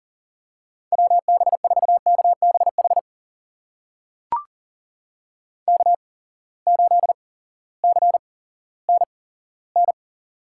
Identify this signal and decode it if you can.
Morse code, heard as a keyed tone, about 42 wpm, spaced as for 18 wpm, 700 Hz. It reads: WB4KBH K8CNN
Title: USA Call Signs - 8x Character Spacing - 40wpm